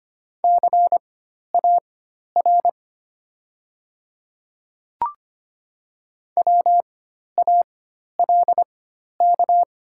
Morse code, heard as a keyed tone, about 25 wpm, spaced as for 12 wpm, 690 Hz. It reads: CAR WALK
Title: 4x Character Spacing - Top 300 Words - 25wpm